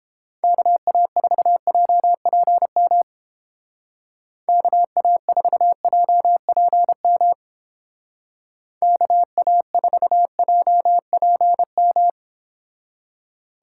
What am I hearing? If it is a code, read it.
Morse code, heard as a keyed tone, about 30 wpm, 700 Hz. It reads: KA4JPM KA4JPM KA4JPM